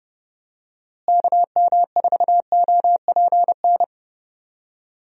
Morse code, heard as a keyed tone, about 30 wpm, 700 Hz. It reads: KM4OPN
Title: USA Call Signs - 6 Characters - Repeated 2x - 30wpm